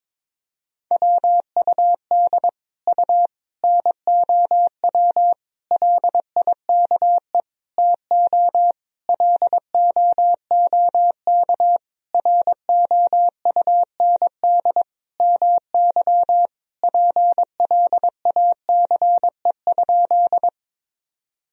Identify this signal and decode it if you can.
Morse code, heard as a keyed tone, about 22 wpm, 700 Hz. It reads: WUD U NOW LIKE TO LOOK ROUND MY PLACE?
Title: Sentences from Top 200 Words and Abbreviations 22wpm